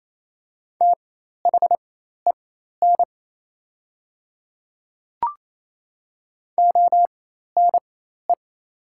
Morse code, heard as a keyed tone, about 28 wpm, spaced as for 13 wpm, 700 Hz. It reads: THEN ONE
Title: 4x Character Spacing - Top 100 Words - 28wpm